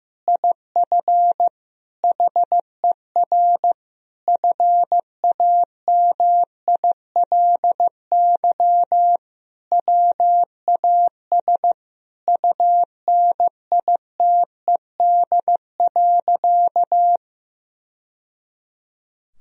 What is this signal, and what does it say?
Morse code, heard as a keyed tone, about 15 wpm, 700 Hz. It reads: IF HER FAMILY WAS UNITED.
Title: Sentences from Top 1000 Words - Review Part 4 - 15wpm